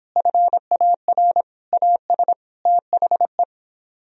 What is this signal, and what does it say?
Morse code, heard as a keyed tone, about 30 wpm, 700 Hz. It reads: FAR AS THE